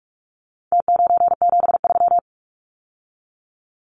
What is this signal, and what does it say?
Morse code, heard as a keyed tone, about 44 wpm, 700 Hz. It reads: T973
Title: Numbers - Contest Style - 4-Digit 45wpm